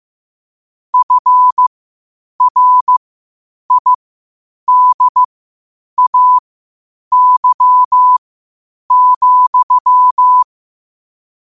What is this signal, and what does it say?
Morse code, heard as a keyed tone, about 15 wpm, spaced as for 8 wpm, 1000 Hz. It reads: FRIDAY,
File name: qotdfiles / 8wpm